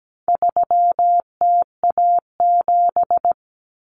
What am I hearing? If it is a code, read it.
Morse code, heard as a keyed tone, about 17 wpm, 700 Hz. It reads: 3TA7
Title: Common Cut Numbers - 4-Digit 17wpm